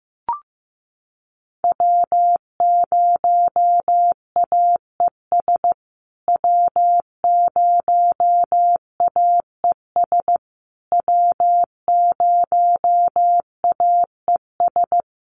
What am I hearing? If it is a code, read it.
Morse code, heard as a keyed tone, about 15 wpm, 700 Hz. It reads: W0AES W0AES W0AES
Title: USA Call Signs Repeated 3x 15wpm